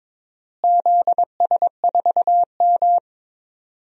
Morse code, heard as a keyed tone, about 22 wpm, 700 Hz. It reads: ZS4M
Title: DXCC-100 Call Signs - 4 Characters - Repeated 2x - 22wpm